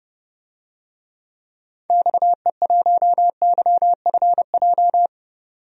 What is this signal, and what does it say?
Morse code, heard as a keyed tone, about 30 wpm, 700 Hz. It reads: XE1YFJ
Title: DXCC-100 Call Signs - 6 Characters - 30wpm